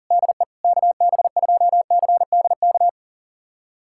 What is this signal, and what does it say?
Morse code, heard as a keyed tone, about 40 wpm, 700 Hz. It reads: DE KB2CDK
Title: USA Callsign DE Callsign 40wpm